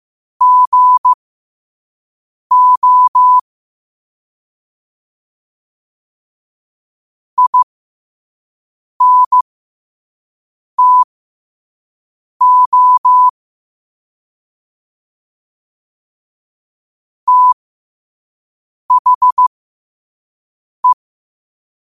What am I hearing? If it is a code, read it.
Morse code, heard as a keyed tone, about 15 wpm, spaced as for 5 wpm, 1000 Hz. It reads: GO INTO THE
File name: qotdfiles / 5wpm